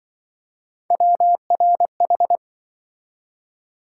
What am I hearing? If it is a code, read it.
Morse code, heard as a keyed tone, about 25 wpm, 700 Hz. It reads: WRH
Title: Speed-Racing - Three Letter-Number Combinations - 20wpm